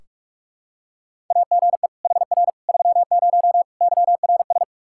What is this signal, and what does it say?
Morse code, heard as a keyed tone, about 46 wpm, 700 Hz. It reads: AGE HR 30 YRS
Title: Practice QSO - Exchange 3 - More Info - 45wpm